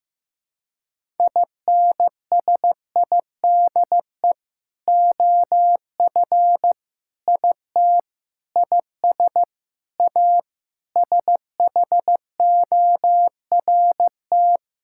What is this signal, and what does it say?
Morse code, heard as a keyed tone, about 16 wpm, 700 Hz. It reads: INSIDE OF IT IS A SHORT
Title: Sentences from Top 1000 Words - Review Part 6 - 15wpm